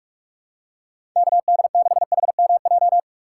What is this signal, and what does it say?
Morse code, heard as a keyed tone, about 46 wpm, 700 Hz. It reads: KD6HMJ
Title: USA Call Signs - 6 Characters - 6x Character Spacing - 45wpm